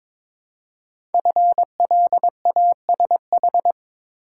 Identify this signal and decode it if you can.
Morse code, heard as a keyed tone, about 22 wpm, 700 Hz. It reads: FLASH
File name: 5 Letter Words - First 3 Letters Given - 22wpm